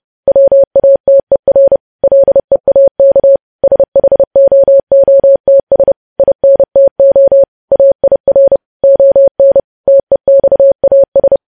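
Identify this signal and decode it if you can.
Morse code, heard as a keyed tone, about 30 wpm, 550 Hz. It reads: WATER LEAK SHOOTS INTO AIR ON TEXAS
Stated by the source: Morse code news headlines practice sent at 30 wpm.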